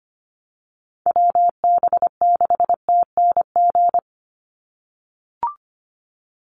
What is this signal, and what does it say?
Morse code, heard as a keyed tone, about 25 wpm, 700 Hz. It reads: WB6TNG
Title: USA Call Signs 25wpm